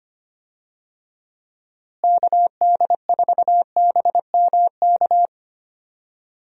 Morse code, heard as a keyed tone, about 25 wpm, 700 Hz. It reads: KD4BMK
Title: USA Call Signs - 6 Characters - Repeated 3x - 25wpm